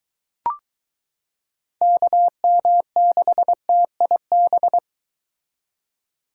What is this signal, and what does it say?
Morse code, heard as a keyed tone, about 23 wpm, 700 Hz. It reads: KM6TIB